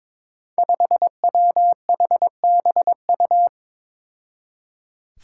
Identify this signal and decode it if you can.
Morse code, heard as a keyed tone, about 22 wpm, 700 Hz. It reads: 5WHBU
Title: Five Challenging Letter-Number Combinations 22wpm